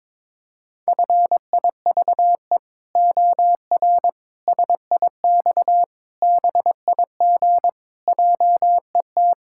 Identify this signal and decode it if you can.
Morse code, heard as a keyed tone, about 22 wpm, 700 Hz. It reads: FIVE OR SIX BIG JET
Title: Pangram Challenge v2 - 22wpm